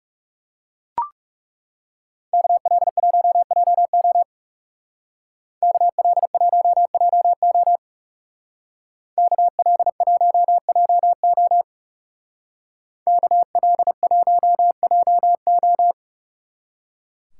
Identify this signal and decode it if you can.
Morse code, heard as a keyed tone, about 36 wpm, 700 Hz. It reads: KL1JO KL1JO KL1JO KL1JO
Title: Speed Racing - USA 2x2 Call Signs 30wpm